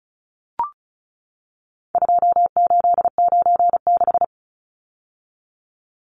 Morse code, heard as a keyed tone, about 34 wpm, 700 Hz. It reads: 2896
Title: Numbers - 4-Digit 35wpm